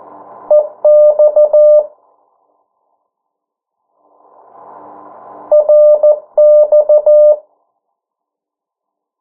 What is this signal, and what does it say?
Morse code, heard as a keyed tone, about 14 wpm, 600 Hz. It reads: EX RX
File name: morse mp3 / 09 urok 8